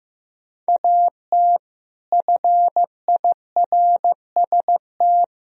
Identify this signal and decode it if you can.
Morse code, heard as a keyed tone, about 15 wpm, 700 Hz. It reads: AT FIRST
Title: Sentences from Top 1000 Words - Review Part 1 - 15wpm